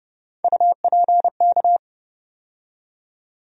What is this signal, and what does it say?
Morse code, heard as a keyed tone, about 30 wpm, 700 Hz. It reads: UPK
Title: USA Call Sign Postfixes - 3 Letters 30wpm